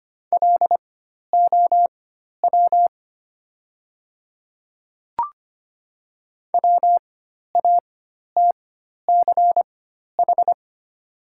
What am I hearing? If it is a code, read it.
Morse code, heard as a keyed tone, about 25 wpm, spaced as for 12 wpm, 700 Hz. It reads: LOW WATCH
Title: 4x Character Spacing - Top 300 Words - 25wpm